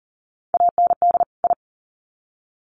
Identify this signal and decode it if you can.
Morse code, heard as a keyed tone, about 40 wpm, 700 Hz. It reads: AND I
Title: Sets of 2 Words 40wpm